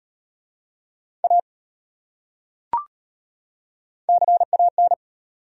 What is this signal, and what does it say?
Morse code, heard as a keyed tone, about 34 wpm, 700 Hz. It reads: A CAN